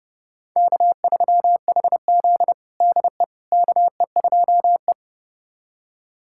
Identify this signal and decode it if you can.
Morse code, heard as a keyed tone, about 31 wpm, 700 Hz. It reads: K3HZ DE KE2E
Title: USA Callsign DE Callsign - 4 Characters - 30wpm